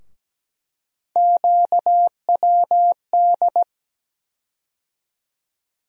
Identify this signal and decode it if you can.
Morse code, heard as a keyed tone, about 17 wpm, 700 Hz. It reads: QWD
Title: USA Call Sign Postfixes - 3 Letters 17wpm